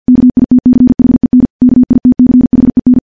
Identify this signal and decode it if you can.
Morse code, heard as a keyed tone, about 50 wpm, 250 Hz. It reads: KITCHEN KITCHEN
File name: morse / morse-training-50WPM